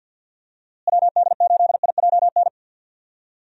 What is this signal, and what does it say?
Morse code, heard as a keyed tone, about 48 wpm, 700 Hz. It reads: WD8IJN
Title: USA Call Signs - 6 Characters - Repeated 2x - 50wpm